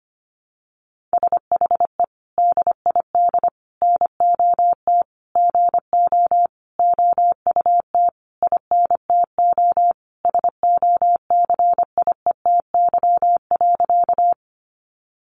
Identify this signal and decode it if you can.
Morse code, heard as a keyed tone, about 25 wpm, 700 Hz. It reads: SHE DID NOT GO OUT INTO SOCIETY.
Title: Sentences from Top 700 Words 25wpm